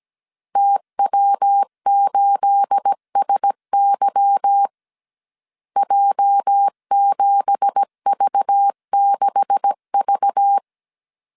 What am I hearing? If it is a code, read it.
Morse code, heard as a keyed tone, about 17 wpm, 800 Hz. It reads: TW8SY J7V6V
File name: CW / DET / 1 9 1500C S A 16wpm 2828